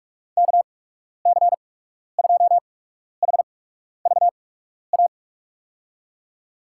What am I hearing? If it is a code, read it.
Morse code, heard as a keyed tone, about 43 wpm, spaced as for 12 wpm, 700 Hz. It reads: KC2HVA
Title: USA Call Signs - 6 Characters - 8x Character Spacing - 45wpm